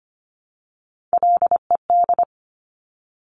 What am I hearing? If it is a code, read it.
Morse code, heard as a keyed tone, about 25 wpm, 700 Hz. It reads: LED
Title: Top Three Letter Combinations in Words 25wpm